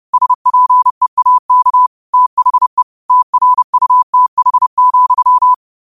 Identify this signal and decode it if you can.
Morse code, heard as a keyed tone, about 30 wpm, 1000 Hz. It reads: SPEAK THE TRUTH,
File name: qotdfiles / 30wpm